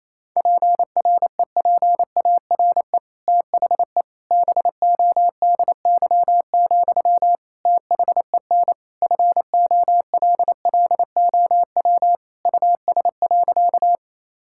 Morse code, heard as a keyed tone, about 28 wpm, 700 Hz. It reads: PREPARE THE BODY, THEN FOLLOW US.
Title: Sentences from Top 1000 Words - Review Part 6 - 28wpm